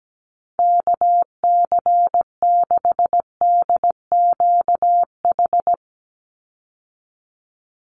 Morse code, heard as a keyed tone, about 17 wpm, 700 Hz. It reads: KC6DQH
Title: USA Call Signs 17wpm